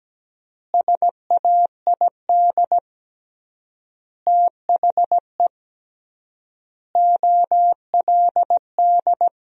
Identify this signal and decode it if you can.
Morse code, heard as a keyed tone, about 17 wpm, 700 Hz. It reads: SAID THE OLD
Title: 3x Word-Spacing - Sentences from Top 500 Words - 17wpm